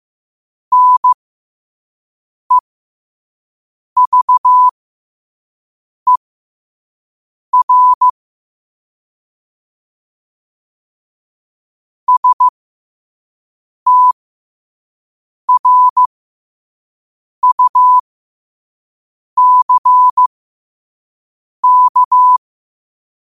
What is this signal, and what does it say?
Morse code, heard as a keyed tone, about 15 wpm, spaced as for 5 wpm, 1000 Hz. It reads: NEVER STRUCK
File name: qotdfiles / 5wpm